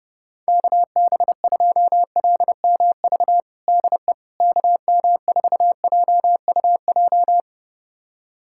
Morse code, heard as a keyed tone, about 30 wpm, 700 Hz. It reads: KB2LMV DE KM4JUJ
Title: USA Callsign DE Callsign - 6 Characters - 30wpm